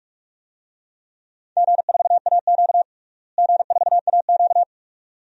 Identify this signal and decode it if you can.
Morse code, heard as a keyed tone, about 46 wpm, 700 Hz. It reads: G4AQ G4AQ
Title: DXCC-100 Call Signs - 4 Characters - Repeated 2x - 45wpm